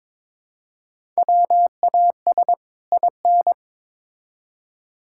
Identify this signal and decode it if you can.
Morse code, heard as a keyed tone, about 22 wpm, 700 Hz. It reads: WAS IN
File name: Build A Phrase - 2-Word Phrases - 22wpm